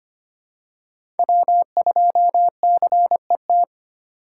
Morse code, heard as a keyed tone, about 25 wpm, 700 Hz. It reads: W2CET
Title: USA Call Signs - 5 Characters - 2x Character Spacing - 25wpm